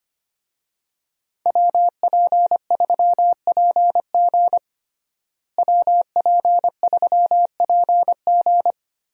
Morse code, heard as a keyed tone, about 25 wpm, 700 Hz. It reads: WP3PG WP3PG
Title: USA Call Signs - 5 Characters - Repeated 3x - 25wpm